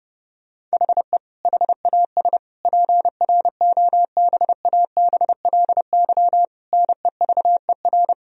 Morse code, heard as a keyed tone, about 32 wpm, 700 Hz. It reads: HE HAS PROBABLY NEVER
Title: Sentences from Top 1000 Words - Review Part 4 - 30wpm